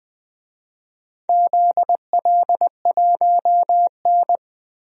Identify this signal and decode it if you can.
Morse code, heard as a keyed tone, about 20 wpm, 700 Hz. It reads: ZL1N
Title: DXCC-100 Call Signs - 4 Characters - Repeated 2x - 20wpm